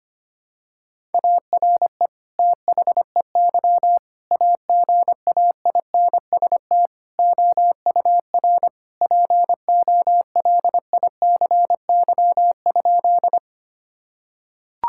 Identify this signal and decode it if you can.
Morse code, heard as a keyed tone, about 25 wpm, 700 Hz. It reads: ARE THEY AGAINST OUR POLICY?
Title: Sentences from Top 1000 Words - Review Part 5 - 25wpm